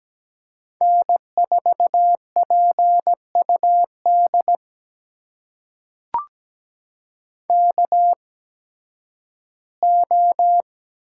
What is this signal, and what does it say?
Morse code, heard as a keyed tone, about 17 wpm, spaced as for 11 wpm, 700 Hz. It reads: N4PUD KO
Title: USA Call Signs - 5 Characters - 8x Character Spacing - 17wpm